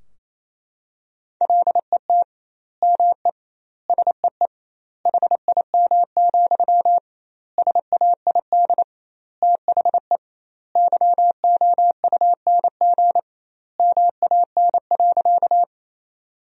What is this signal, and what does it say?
Morse code, heard as a keyed tone, about 28 wpm, 700 Hz. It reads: LET ME SEE HIM, SAID THE YOUNG MAN.
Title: 2x Word-Spacing - Sentences from Top 400 Words - 28wpm